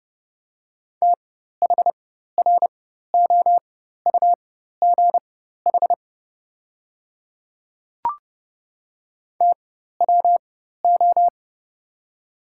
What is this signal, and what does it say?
Morse code, heard as a keyed tone, about 29 wpm, spaced as for 14 wpm, 700 Hz. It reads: THROUGH TWO
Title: 4x Character Spacing - Top 200 Words - 30wpm